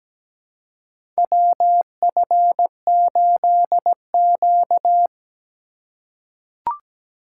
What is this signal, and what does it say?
Morse code, heard as a keyed tone, about 17 wpm, 700 Hz. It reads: WF8Q
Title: USA Call Signs - 4 Characters - 2x Character Spacing - 17wpm